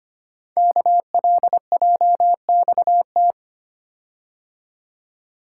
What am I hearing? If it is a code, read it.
Morse code, heard as a keyed tone, about 25 wpm, 700 Hz. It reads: KLJXT
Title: Code Groups - 5 Letter:Numbers - 25wpm